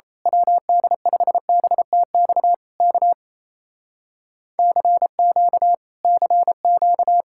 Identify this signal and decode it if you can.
Morse code, heard as a keyed tone, about 27 wpm, 700 Hz. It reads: WD5BTX K CQ CQ
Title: Speed-Racing Common QSO Phrases Encore - 28wpm